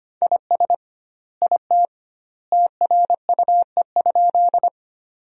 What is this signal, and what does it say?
Morse code, heard as a keyed tone, about 25 wpm, 700 Hz. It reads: IS IT TRUE?
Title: 2x Word-Spacing - Sentences from Top 400 Words - 25wpm